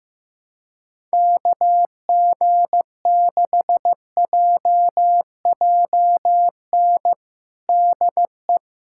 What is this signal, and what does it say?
Morse code, heard as a keyed tone, about 15 wpm, 700 Hz. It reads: KG6JJN DE
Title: USA Callsign DE Callsign - 6 Characters - 15wpm